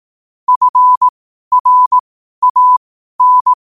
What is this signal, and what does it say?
Morse code, heard as a keyed tone, about 18 wpm, spaced as for 12 wpm, 1000 Hz. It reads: FRAN
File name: qotdfiles / 12wpm